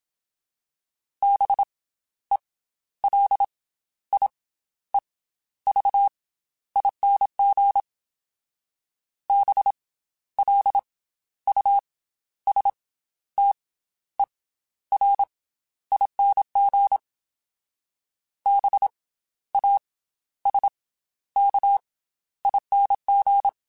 Morse code, heard as a keyed tone, about 27 wpm, 800 Hz. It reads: B E L I E V ING B L U S T E R ING B A S K ING
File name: Special INGs / Special INGs-mp3s / Slow ing @25